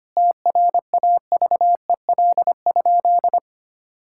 Morse code, heard as a keyed tone, about 25 wpm, 700 Hz. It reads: TRAVEL?
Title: Sentences from Top 1000 Words - Review Part 6 - 25wpm